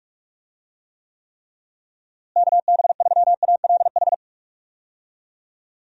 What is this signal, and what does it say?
Morse code, heard as a keyed tone, about 48 wpm, 700 Hz. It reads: KB3ALH
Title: USA Call Signs - 6 Characters - Repeated 3x - 45wpm